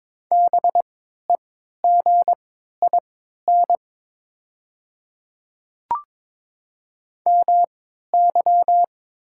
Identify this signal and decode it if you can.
Morse code, heard as a keyed tone, about 22 wpm, spaced as for 13 wpm, 700 Hz. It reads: BEGIN MY
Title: 3x Character Spacing - Top 500 Words - 22wpm